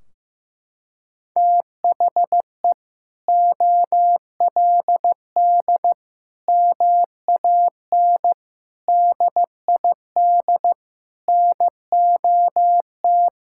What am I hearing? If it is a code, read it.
Morse code, heard as a keyed tone, about 15 wpm, 700 Hz. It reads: THE OLD MAN DID NOT
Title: Sentences from Top 1600 Words 15wpm